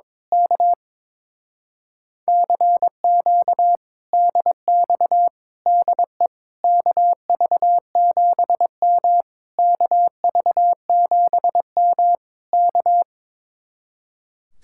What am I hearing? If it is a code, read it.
Morse code, heard as a keyed tone, about 22 wpm, 700 Hz. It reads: K CQ DX DE KV7M KV7M K
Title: Speed-Racing Common QSO Phrases - 22wpm